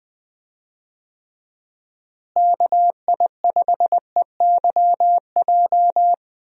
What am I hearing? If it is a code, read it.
Morse code, heard as a keyed tone, about 20 wpm, 700 Hz. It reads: KI5EYJ
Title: USA Call Signs - 6 Characters - Repeated 2x - 20wpm